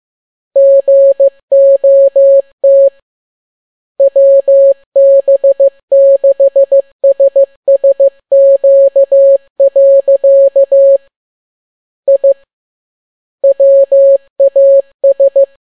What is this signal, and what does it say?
Morse code, heard as a keyed tone, about 15 wpm, 550 Hz. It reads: GOT WB6SSQ. I WAS
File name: n6tt p1 15C13